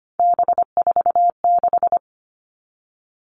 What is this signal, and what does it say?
Morse code, heard as a keyed tone, about 25 wpm, 700 Hz. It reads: B46
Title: All Cut Numbers - 3-Digit 25wpm